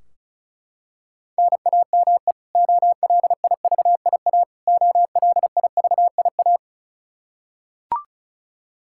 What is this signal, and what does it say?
Morse code, heard as a keyed tone, about 36 wpm, 700 Hz. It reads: NAME OLIVIA OLIVIA
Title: Practice QSO - Exchange 1 - Beginning - 35wpm